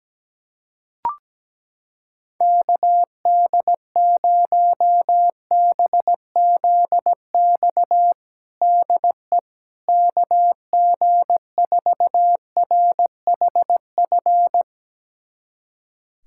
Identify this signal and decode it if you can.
Morse code, heard as a keyed tone, about 17 wpm, 700 Hz. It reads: KD0BZX DE KG4RHF
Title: USA Callsign DE Callsign - 6 Characters - 17wpm